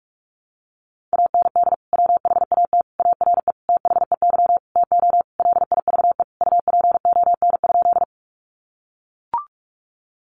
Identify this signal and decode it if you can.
Morse code, heard as a keyed tone, about 44 wpm, 700 Hz. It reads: AND WHAT ARE THEY TO LIVE UPON?
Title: Sentences from Top 1100 Words 45wpm